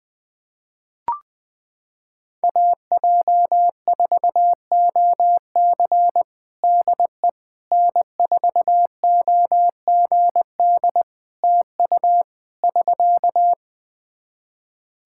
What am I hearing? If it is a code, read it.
Morse code, heard as a keyed tone, about 20 wpm, 700 Hz. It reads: AJ4OC DE N4OGD TU <SK>
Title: Practice QSO - Exchange 4 - Ending - 20wpm